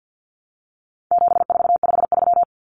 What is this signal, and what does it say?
Morse code, heard as a keyed tone, about 50 wpm, 700 Hz. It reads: 7453
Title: Numbers - 4-Digit 50wpm